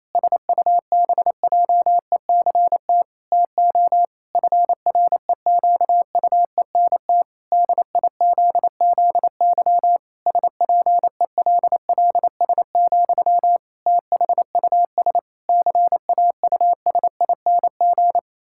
Morse code, heard as a keyed tone, about 28 wpm, 700 Hz. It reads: SUBJECT TO FREQUENT DIZZY SPELLS, THUS CAUSING
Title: Pangram Challenge v2 - 28wpm